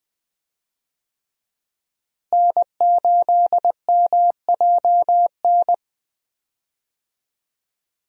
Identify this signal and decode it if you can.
Morse code, heard as a keyed tone, about 20 wpm, 700 Hz. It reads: N8MJN
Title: USA Call Signs - 5 Characters - Repeated 3x - 20wpm